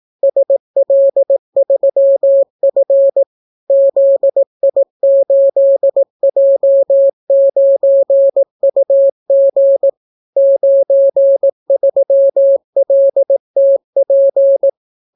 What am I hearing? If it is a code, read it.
Morse code, heard as a keyed tone, about 18 wpm, 550 Hz. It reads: SL3F ZI8J9UG 93LTP